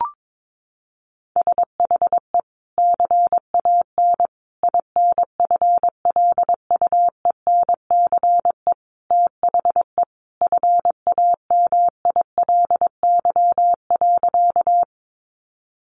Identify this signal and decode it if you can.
Morse code, heard as a keyed tone, about 22 wpm, 700 Hz. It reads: SHE CAN INFLUENCE THE FAMILY.
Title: Sentences from Top 1100 Words 22wpm